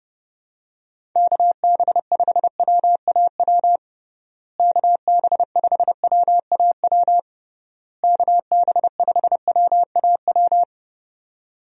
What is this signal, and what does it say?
Morse code, heard as a keyed tone, about 32 wpm, 700 Hz. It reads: KB5WAW KB5WAW KB5WAW
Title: USA Call Signs - 6 Characters - Repeated 3x - 30wpm